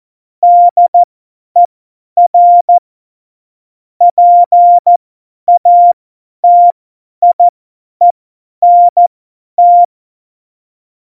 Morse code, heard as a keyed tone, about 14 wpm, spaced as for 10 wpm, 700 Hz. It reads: DER PATIENT
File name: Morsekurs / Witze / Witz 08. September 2020, 50 F70